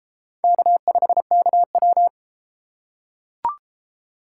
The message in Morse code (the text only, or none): K5KW